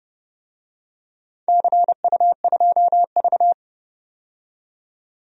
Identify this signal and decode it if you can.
Morse code, heard as a keyed tone, about 30 wpm, 700 Hz. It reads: CU2V
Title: DXCC-100 Call Signs - 4 Characters - 30wpm